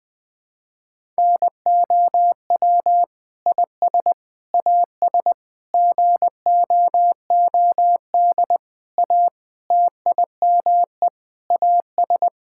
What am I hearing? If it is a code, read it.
Morse code, heard as a keyed tone, about 20 wpm, 700 Hz. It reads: NOW IS AS GOOD A TIME AS
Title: Sentences with 8-Word Phrases - 20wpm